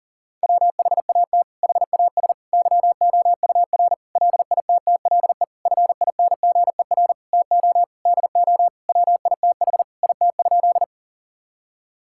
Morse code, heard as a keyed tone, about 40 wpm, 700 Hz. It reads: WHAT HAS YOUR LITTLE FINGER TO DO WITH IT?
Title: Sentences from Top 1600 Words 40wpm